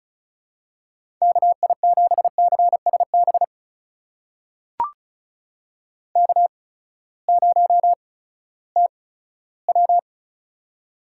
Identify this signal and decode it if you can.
Morse code, heard as a keyed tone, about 36 wpm, spaced as for 18 wpm, 700 Hz. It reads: KI7CSB K0TW
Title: USA Call Signs - 8x Character Spacing - 35wpm